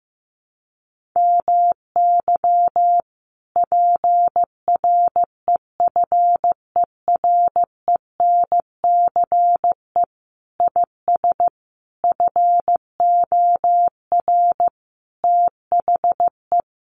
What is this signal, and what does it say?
Morse code, heard as a keyed tone, about 15 wpm, 700 Hz. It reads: MY PREFERENCE IS FOR THE
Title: Sentences from Top 1500 Words 15wpm